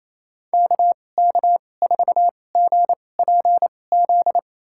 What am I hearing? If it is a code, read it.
Morse code, heard as a keyed tone, about 28 wpm, spaced as for 20 wpm, 700 Hz. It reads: KK4GPZ
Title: USA Call Signs - 6 Characters - 2x Character Spacing - 28wpm